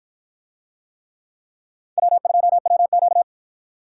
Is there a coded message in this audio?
Yes: Morse code from 2.0 to 3.2 s, heard as a keyed tone, about 53 wpm, 700 Hz. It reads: W2WQ